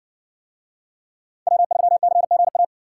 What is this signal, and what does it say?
Morse code, heard as a keyed tone, about 53 wpm, 700 Hz. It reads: W3CGA